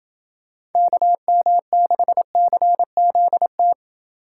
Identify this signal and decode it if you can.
Morse code, heard as a keyed tone, about 28 wpm, 700 Hz. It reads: KM6CZT